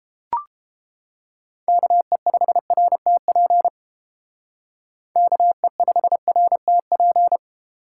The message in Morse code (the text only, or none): KE5RTP KE5RTP